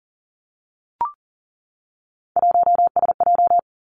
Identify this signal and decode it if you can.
Morse code, heard as a keyed tone, about 42 wpm, 700 Hz. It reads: 1SJ
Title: Three Challenging Letter-Number Combinations 40wpm